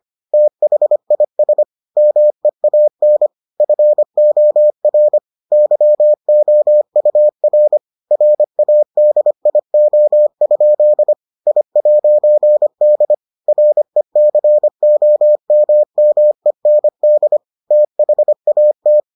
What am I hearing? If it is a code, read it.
Morse code, heard as a keyed tone, about 25 wpm, 600 Hz. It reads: THIS MEAN FOR YOUR RADIO? I'D RECOMMEND THAT